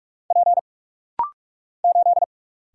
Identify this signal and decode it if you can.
Morse code, heard as a keyed tone, about 44 wpm, 700 Hz. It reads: P 8
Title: Single Letter-Number Alt - Rapid-Fire 45wpm